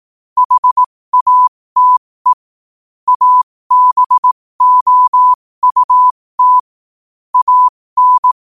Morse code, heard as a keyed tone, about 18 wpm, spaced as for 15 wpm, 1000 Hz. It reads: HATE ABOUT AN
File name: qotdfiles / 15wpm